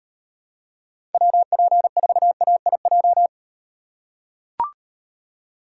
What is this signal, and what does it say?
Morse code, heard as a keyed tone, about 38 wpm, 700 Hz. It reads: WP4AIJ